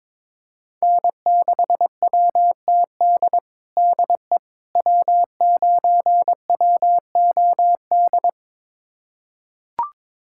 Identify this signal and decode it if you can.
Morse code, heard as a keyed tone, about 22 wpm, 700 Hz. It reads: N6WTD DE W9WOD
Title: USA Callsign DE Callsign - 5 Characters - 22wpm